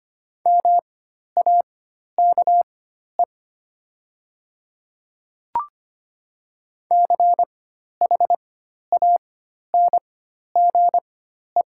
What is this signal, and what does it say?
Morse code, heard as a keyed tone, about 25 wpm, spaced as for 12 wpm, 700 Hz. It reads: MAKE CHANGE
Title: 4x Character Spacing - Top 200 Words - 25wpm